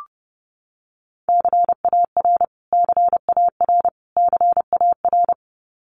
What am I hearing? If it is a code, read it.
Morse code, heard as a keyed tone, about 30 wpm, 700 Hz. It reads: CAR CAR CAR
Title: Top 500 Words Repeated 30wpm